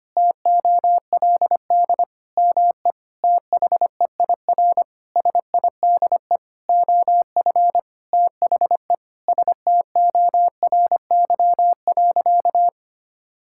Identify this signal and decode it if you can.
Morse code, heard as a keyed tone, about 25 wpm, 700 Hz. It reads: TOLD ME THEIR SIDE OF THE STORY.
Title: Sentences from Top 1000 Words - Review Part 5 - 25wpm